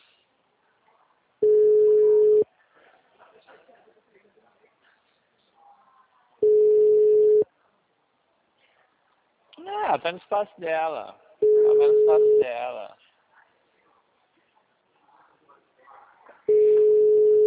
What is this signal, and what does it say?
Ringback tone (Europe, 425 Hz), with speech.